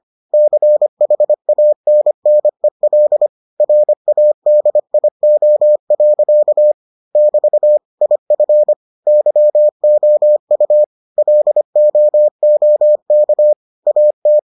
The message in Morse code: CHANNEL RADIO. <BT> IF YOU LOOK AT

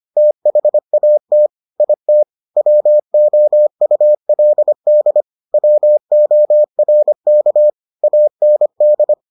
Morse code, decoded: THAT IT WOULD WORK AND

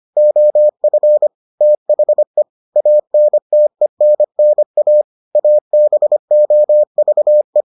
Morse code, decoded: OF THE ANTENNA ABOVE

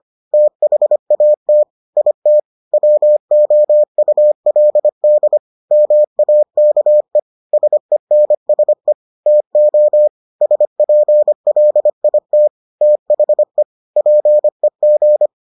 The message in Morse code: THAT IT WOULD MAKE SENSE TO SPLIT THE PEG

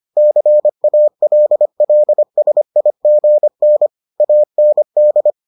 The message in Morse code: CALLSIGN AND

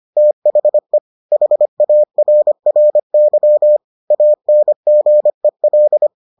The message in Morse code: THE HARRY ANGEL

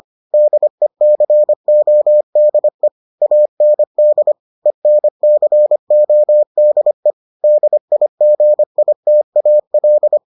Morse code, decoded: DECODE AND ENCODE DIGITAL

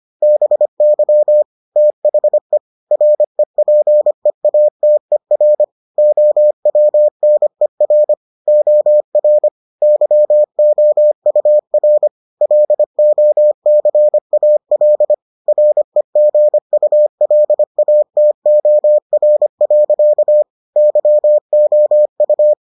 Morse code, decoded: BY THE REPEATER OWNER OR YOUR LOCAL REGULATOR. YOU

